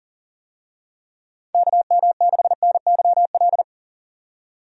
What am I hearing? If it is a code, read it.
Morse code, heard as a keyed tone, about 40 wpm, 700 Hz. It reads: KM6NYL
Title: USA Call Signs - 6 Characters - 40wpm